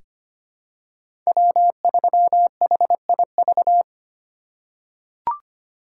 Morse code, heard as a keyed tone, about 25 wpm, 700 Hz. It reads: W3HIV
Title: USA Call Signs - 5 Characters - 2x Character Spacing - 25wpm